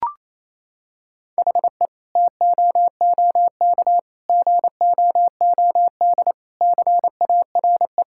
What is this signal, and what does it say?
Morse code, heard as a keyed tone, about 28 wpm, 700 Hz. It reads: HE TOOK GOOD CARE
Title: Sentences from Top 1000 Words - Review Part 5 - 28wpm